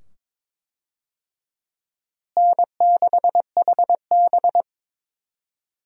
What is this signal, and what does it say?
Morse code, heard as a keyed tone, about 22 wpm, 700 Hz. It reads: N6HB
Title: USA Call Signs - 4 Characters - Repeated 2x - 22wpm